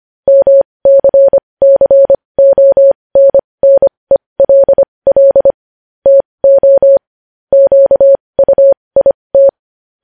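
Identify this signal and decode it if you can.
Morse code, heard as a keyed tone, about 25 wpm, spaced as for 20 wpm, 550 Hz. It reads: MCCONNELL TO QUIT
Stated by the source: Morse code news headlines practice at 20 wpm.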